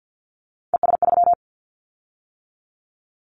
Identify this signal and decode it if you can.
Morse code, heard as a keyed tone, about 50 wpm, 700 Hz. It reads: ES3
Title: Callsign Prefixes for Top 50 DXCC - 50wpm